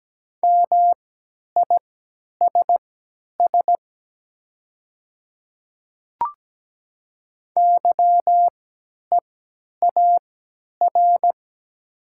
Morse code, heard as a keyed tone, about 17 wpm, spaced as for 10 wpm, 700 Hz. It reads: MISS YEAR
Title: 3x Character Spacing - Top 500 Words - 17wpm